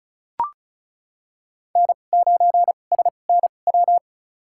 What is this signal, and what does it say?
Morse code, heard as a keyed tone, about 36 wpm, spaced as for 25 wpm, 700 Hz. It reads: N9SNW
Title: USA Call Signs - 5 Characters - 2x Character Spacing - 35wpm